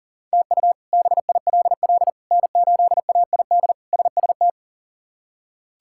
Morse code, heard as a keyed tone, about 40 wpm, 700 Hz. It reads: TU BILL N8AID SST
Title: K1USN SST QSO Practice - 40wpm